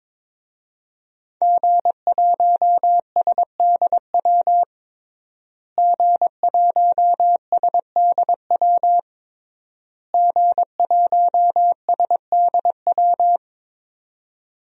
Morse code, heard as a keyed tone, about 22 wpm, 700 Hz. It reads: G1SDW G1SDW G1SDW
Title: DXCC-100 Call Signs - 5 Characters - Repeated 3x - 22wpm